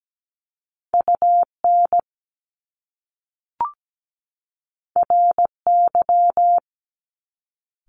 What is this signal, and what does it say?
Morse code, heard as a keyed tone, about 17 wpm, 700 Hz. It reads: UN RY
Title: Top Two Letter Combinations in Words 17wpm